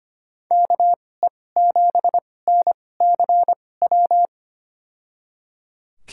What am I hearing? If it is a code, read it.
Morse code, heard as a keyed tone, about 25 wpm, spaced as for 18 wpm, 700 Hz. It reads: KE7NCW
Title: USA Call Signs - 6 Characters - 2x Character Spacing - 25wpm